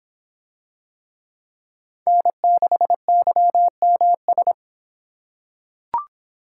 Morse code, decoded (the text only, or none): N6YMS